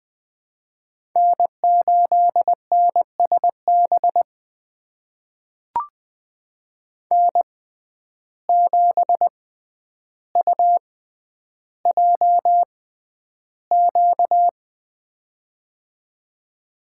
Morse code, heard as a keyed tone, about 20 wpm, spaced as for 10 wpm, 700 Hz. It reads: N8NSB N7UJQ
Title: USA Call Signs - 5 Characters - 6x Character Spacing - 20wpm